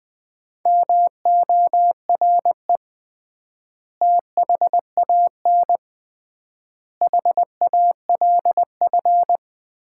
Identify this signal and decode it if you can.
Morse code, heard as a keyed tone, about 20 wpm, 700 Hz. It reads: MORE THAN HALF
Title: 3x Word-Spacing - Sets of 5 Words - 20wpm